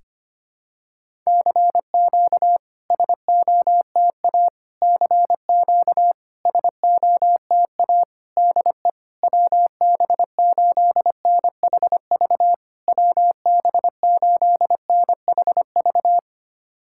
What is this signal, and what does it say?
Morse code, heard as a keyed tone, about 25 wpm, 700 Hz. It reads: CQ SOTA CQ SOTA DE WB8NHV WB8NHV
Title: SOTA QSO Practice 25wpm